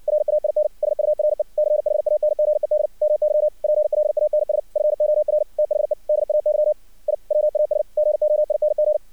Morse code, heard as a keyed tone, about 62 wpm, 600 Hz. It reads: RAIN SURE CHANGED MY PLANS FOR THE DAY I WAS GOING
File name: archived / CRSnet